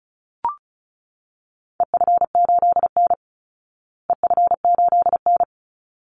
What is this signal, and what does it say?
Morse code, heard as a keyed tone, about 34 wpm, 700 Hz. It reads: EF8N EF8N
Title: Call Signs Repeated 3x - 35wpm